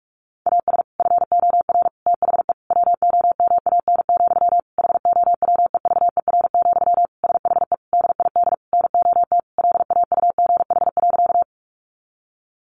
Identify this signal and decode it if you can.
Morse code, heard as a keyed tone, about 44 wpm, 700 Hz. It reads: AS FOR THE WOMAN, HOWEVER, SHE DID NOT LAUGH.
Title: Sentences from Top 1100 Words 45wpm